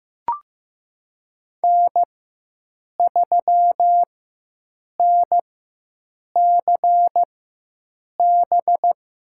Morse code, heard as a keyed tone, about 16 wpm, spaced as for 7 wpm, 700 Hz. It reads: N3NCB
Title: USA Call Signs - 4x Character Spacing - 15wpm